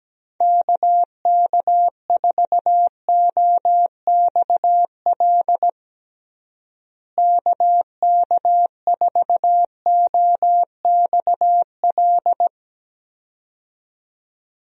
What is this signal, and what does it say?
Morse code, heard as a keyed tone, about 17 wpm, 700 Hz. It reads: KK4OXL KK4OXL
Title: USA Call Signs - 6 Characters - Repeated 2x - 17wpm